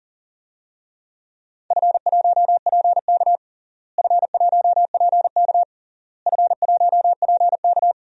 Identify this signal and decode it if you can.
Morse code, heard as a keyed tone, about 40 wpm, 700 Hz. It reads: F1PK F1PK F1PK
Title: DXCC-100 Call Signs - 4 Characters - Repeated 3x - 40wpm